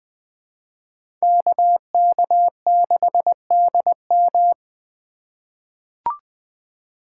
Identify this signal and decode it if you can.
Morse code, heard as a keyed tone, about 20 wpm, 700 Hz. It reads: KK6DM
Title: USA Call Signs - 4x Character Spacing - 20wpm